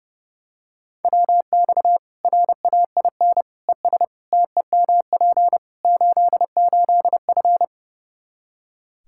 Morse code, heard as a keyed tone, about 30 wpm, 700 Hz. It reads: WX RAIN ES TEMP 88F
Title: Practice QSO - Exchange 2 - Info - 30wpm